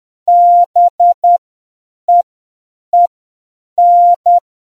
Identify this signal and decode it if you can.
Morse code, heard as a keyed tone, about 10 wpm, spaced as for 7 wpm, 700 Hz. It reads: BEEN